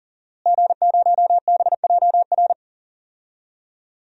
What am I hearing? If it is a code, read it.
Morse code, heard as a keyed tone, about 40 wpm, 700 Hz. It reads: G0BJR
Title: DXCC-100 Call Signs - 5 Characters - Repeated 2x - 40wpm